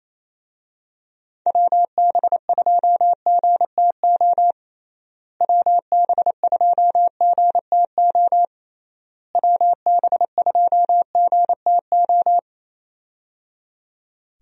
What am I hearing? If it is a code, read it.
Morse code, heard as a keyed tone, about 28 wpm, 700 Hz. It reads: WB2GTO WB2GTO WB2GTO
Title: USA Call Signs - 6 Characters - Repeated 3x - 28wpm